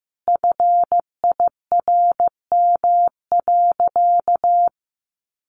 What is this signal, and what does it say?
Morse code, heard as a keyed tone, about 15 wpm, 700 Hz. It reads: FIRM.
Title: Sentences from Top 1400 Words 15wpm